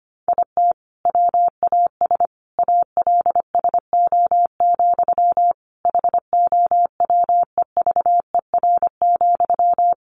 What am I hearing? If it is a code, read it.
Morse code, heard as a keyed tone, about 25 wpm, 700 Hz. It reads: IT WAS ALSO, HOWEVER,
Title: Sentences from Top 1400 Words 25wpm